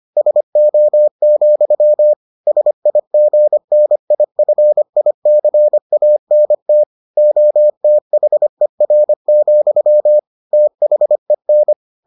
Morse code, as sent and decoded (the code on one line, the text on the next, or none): ... --- --..-- / ... .. --. -. .. ..-. .. -.-. .- -. - / --- - .... .-. --..-- / - .... -.
SO, SIGNIFICANT OTHER, THEN